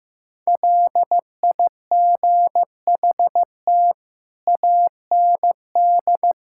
Morse code, heard as a keyed tone, about 15 wpm, 700 Hz. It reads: LIGHT AND
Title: Sentences from Top 1000 Words - Review Part 5 - 15wpm